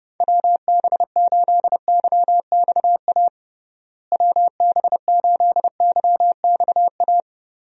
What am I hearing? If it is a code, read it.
Morse code, heard as a keyed tone, about 32 wpm, 700 Hz. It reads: WB8YXA WB8YXA
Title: USA Call Signs - 6 Characters - Repeated 2x - 30wpm